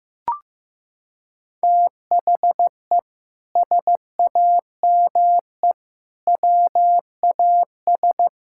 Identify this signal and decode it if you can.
Morse code, heard as a keyed tone, about 16 wpm, 700 Hz. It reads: THE SAME WAS
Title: Sentences from Top 1000 Words - Review Part 6 - 15wpm